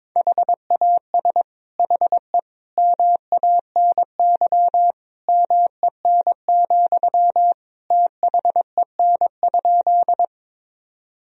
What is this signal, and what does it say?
Morse code, heard as a keyed tone, about 22 wpm, 700 Hz. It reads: HAS HE MANY MEN, THEN?
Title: Sentences from Top 1000 Words - Review Part 1 - 22wpm